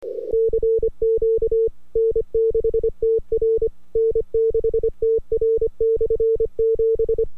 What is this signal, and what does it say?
Morse code, heard as a keyed tone, about 25 wpm, 450 Hz. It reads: CQ N6TR N6TR/7
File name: n6tr / RX9TX 2-Jan-2009 1419z